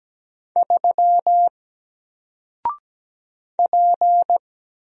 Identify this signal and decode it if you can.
Morse code, heard as a keyed tone, about 17 wpm, 700 Hz. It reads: 3 P
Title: Single Character (Letter:Number:Punctuation:Prosign) 17wpm